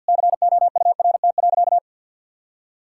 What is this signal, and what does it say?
Morse code, heard as a keyed tone, about 50 wpm, 700 Hz. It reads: COURT.
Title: Sentences from Top 1600 Words 50wpm